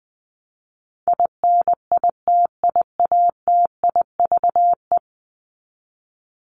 Morse code, heard as a keyed tone, about 20 wpm, 700 Hz. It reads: INITIATIVE
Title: Top 1751-2000 Words 20wpm